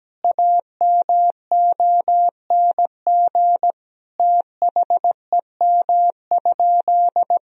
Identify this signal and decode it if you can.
Morse code, heard as a keyed tone, about 17 wpm, 700 Hz. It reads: AMONG THEM?
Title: Sentences from Top 1000 Words - Review Part 4 - 17wpm